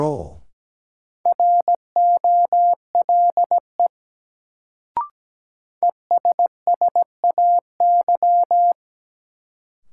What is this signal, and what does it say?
Morse code, heard as a keyed tone, about 17 wpm, 700 Hz. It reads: ROLE ESSAY
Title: Sentences from Top 1500 Words - New Word Drill Alt - 17wpm